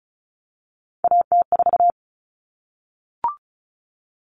Morse code, heard as a keyed tone, about 34 wpm, 700 Hz. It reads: AT4
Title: Common Cut Numbers - 3-Digit 35wpm